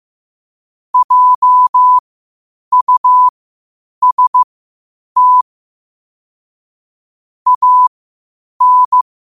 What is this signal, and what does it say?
Morse code, heard as a keyed tone, about 15 wpm, spaced as for 8 wpm, 1000 Hz. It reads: JUST AN
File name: qotdfiles / 8wpm